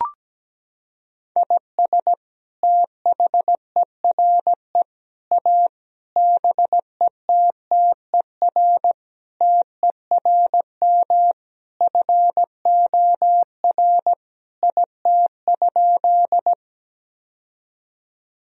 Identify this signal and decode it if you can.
Morse code, heard as a keyed tone, about 17 wpm, 700 Hz. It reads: IS THERE A BETTER TERM FOR IT?
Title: Sentences from Top 1000 Words - Review Part 4 - 17wpm